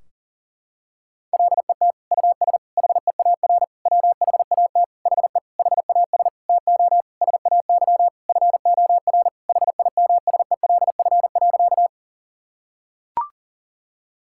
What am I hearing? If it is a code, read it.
Morse code, heard as a keyed tone, about 40 wpm, 700 Hz. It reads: LET US HEAR WHAT HE HAS TO SAY FOR HIMSELF.
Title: Sentences from Top 1600 Words 40wpm